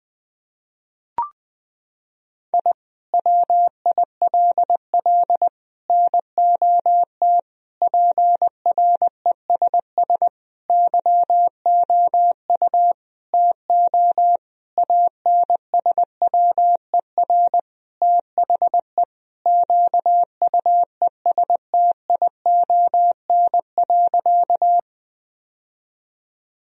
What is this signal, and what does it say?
Morse code, heard as a keyed tone, about 20 wpm, 700 Hz. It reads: I WILL NOT PRESS YOU TO ANSWER THE QUESTION.
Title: Sentences from Top 1000 Words - Review Part 1 - 20wpm